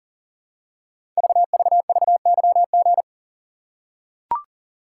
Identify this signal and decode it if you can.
Morse code, heard as a keyed tone, about 42 wpm, 700 Hz. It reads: VVVYG
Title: Code Groups - 5 Letters - 40wpm